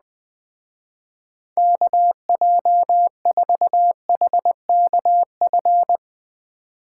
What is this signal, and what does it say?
Morse code, heard as a keyed tone, about 20 wpm, 700 Hz. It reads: KJ4HKF
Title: Speed Racing - USA 2x3 Call Signs 17wpm